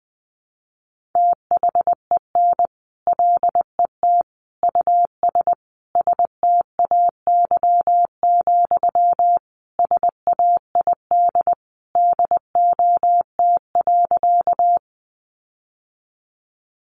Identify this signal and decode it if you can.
Morse code, heard as a keyed tone, about 20 wpm, 700 Hz. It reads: THEN LET US STAY, SAID DOT.
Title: Sentences from Top 1500 Words 20wpm